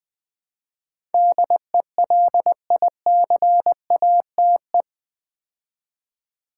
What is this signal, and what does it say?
Morse code, heard as a keyed tone, about 20 wpm, 700 Hz. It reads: DELICATE
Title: Build A Word - 4 Letters At A Time - 8-Letter Words - 20wpm